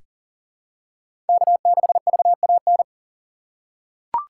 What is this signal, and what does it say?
Morse code, heard as a keyed tone, about 40 wpm, 700 Hz. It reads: K6VAN
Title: USA Call Signs - 5 Characters - 3x Character Spacing - 40wpm